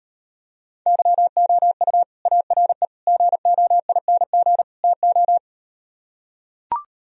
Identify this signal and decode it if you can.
Morse code, heard as a keyed tone, about 39 wpm, 700 Hz. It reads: YOU ARE GOING TO